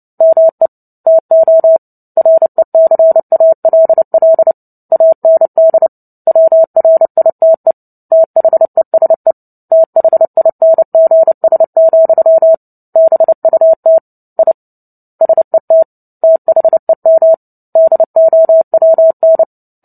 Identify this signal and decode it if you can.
Morse code, heard as a keyed tone, about 30 wpm, 650 Hz. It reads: ME TO RECALL AND WRITE THESE THINGS, BUT I SET THEM DOWN
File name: Book Two / 03-Chapter 3 - 29 WPM